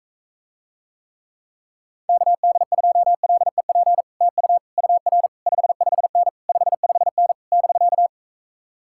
Morse code, heard as a keyed tone, about 45 wpm, 700 Hz. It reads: KD2LEP TU UR 55N 55N <BK>